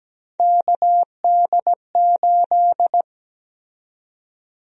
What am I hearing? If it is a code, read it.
Morse code, heard as a keyed tone, about 17 wpm, 700 Hz. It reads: KD8
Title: USA Call Sign Prefixes - Repeated 2x - 17wpm